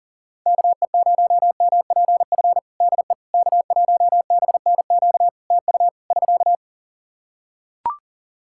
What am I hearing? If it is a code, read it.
Morse code, heard as a keyed tone, about 40 wpm, 700 Hz. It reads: KE0MPF DE K1BNQ TU <SK>
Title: Practice QSO - Exchange 4 - Ending - 40wpm